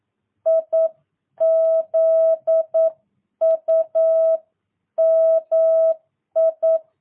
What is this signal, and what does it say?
Morse code, heard as a keyed tone, about 9 wpm, 650 Hz. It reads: IZUMI
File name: jh8cbh / 2022cw